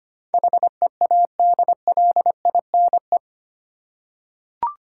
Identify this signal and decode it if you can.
Morse code, heard as a keyed tone, about 25 wpm, 700 Hz. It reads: HEADLINE
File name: Eight Letter Words - 25wpm